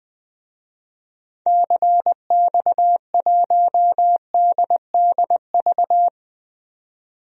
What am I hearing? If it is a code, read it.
Morse code, heard as a keyed tone, about 20 wpm, 700 Hz. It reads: CX1DDV
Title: DXCC-100 Call Signs - 6 Characters - 20wpm